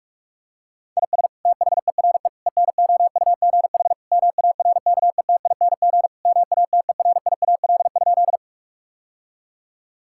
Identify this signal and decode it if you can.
Morse code, heard as a keyed tone, about 47 wpm, 700 Hz. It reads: IS THERE ENOUGH MARKETING MATERIAL?
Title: Sentences from Top 1000 Words - Review Part 5 - 45wpm